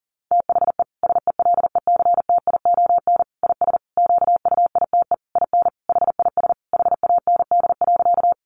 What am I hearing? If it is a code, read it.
Morse code, heard as a keyed tone, about 40 wpm, 700 Hz. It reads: THE SELECTION IS QUITE IN HIS HAND.
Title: Sentences from Top 1400 Words 40wpm